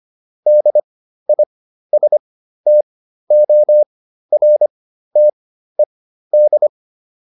Morse code, heard as a keyed tone, about 25 wpm, spaced as for 13 wpm, 600 Hz. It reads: DISTORTED